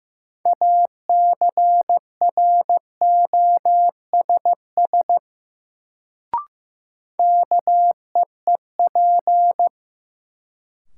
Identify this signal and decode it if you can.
Morse code, heard as a keyed tone, about 15 wpm, 700 Hz. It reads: ACROSS KEEP
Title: Sets of 2 Words - New Word Drill - 15wpm